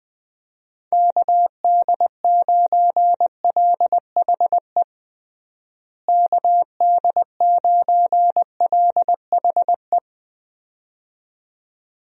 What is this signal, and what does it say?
Morse code, heard as a keyed tone, about 20 wpm, 700 Hz. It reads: KD9LHE KD9LHE
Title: USA Call Signs - 6 Characters - Repeated 2x - 20wpm